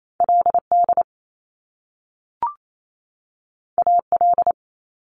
Morse code, heard as a keyed tone, about 28 wpm, 700 Hz. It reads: L D A L
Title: Top Two Letter Combinations in Words 28wpm